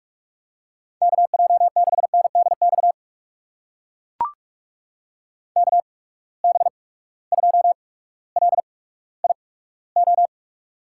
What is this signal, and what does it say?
Morse code, heard as a keyed tone, about 44 wpm, spaced as for 19 wpm, 700 Hz. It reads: KJ6NDX KB2LIO
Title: USA Call Signs - 6 Characters - 8x Character Spacing - 45wpm